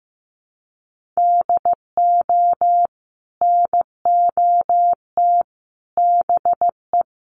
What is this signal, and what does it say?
Morse code, heard as a keyed tone, about 15 wpm, 700 Hz. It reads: DO NOT BE
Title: Sentences from Top 1500 Words 15wpm